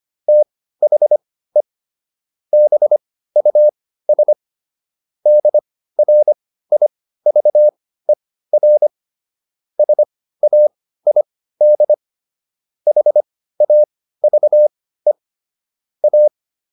Morse code, heard as a keyed tone, about 25 wpm, spaced as for 15 wpm, 600 Hz. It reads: THE BUS DRIVER SAID HAVE A